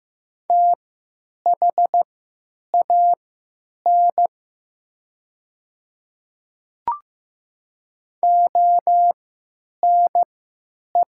Morse code, heard as a keyed tone, about 15 wpm, spaced as for 9 wpm, 700 Hz. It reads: THAN ONE
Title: 3x Character Spacing - Top 100 Words - 15wpm